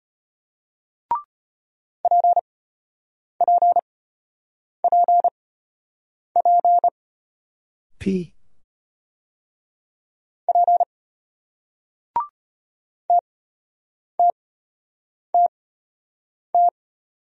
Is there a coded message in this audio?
Yes: Morse code from 2.0 to 16.7 s, heard as a keyed tone, about 33 wpm, spaced as for 8 wpm, 700 Hz. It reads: PPPP P TTTT